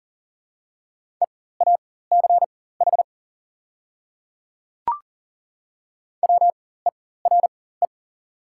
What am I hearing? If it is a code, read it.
Morse code, heard as a keyed tone, about 40 wpm, spaced as for 19 wpm, 690 Hz. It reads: EACH WERE
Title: 4x Character Spacing - Top 100 Words - 40wpm